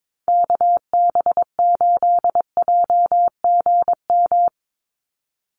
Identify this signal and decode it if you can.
Morse code, heard as a keyed tone, about 22 wpm, 700 Hz. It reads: KB8JGM
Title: USA Call Signs 22wpm